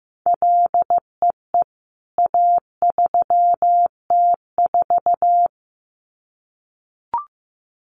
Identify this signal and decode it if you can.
Morse code, heard as a keyed tone, about 15 wpm, 700 Hz. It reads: LEE A3T4
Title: CWT Exchange 15wpm